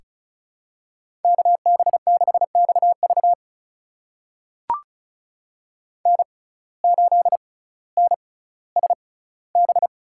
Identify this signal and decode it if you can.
Morse code, heard as a keyed tone, about 36 wpm, spaced as for 18 wpm, 700 Hz. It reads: KB6XV N8NSB
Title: USA Call Signs - 5 Characters - 6x Character Spacing - 35wpm